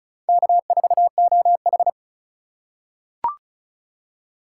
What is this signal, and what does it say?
Morse code, heard as a keyed tone, about 34 wpm, 700 Hz. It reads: K4OH
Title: USA Call Signs - 3x Character Spacing - 35wpm